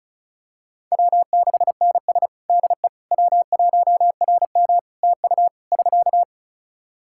Morse code, heard as a keyed tone, about 36 wpm, 700 Hz. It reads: W6NS DE W1RM TU <SK>
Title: Practice QSO - Exchange 4 - Ending - 35wpm